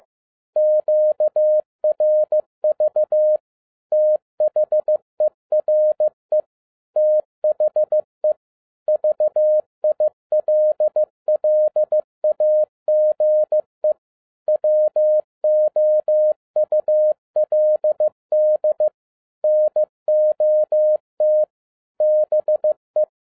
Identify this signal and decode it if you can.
Morse code, heard as a keyed tone, about 15 wpm, 610 Hz. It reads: QRV THERE THE VILLAGE WOULD NOT BE
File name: 2011-10-mp3 / n02-FIELD DAY CONTEST QRV FROM SAKAE VILLAGE 15wpm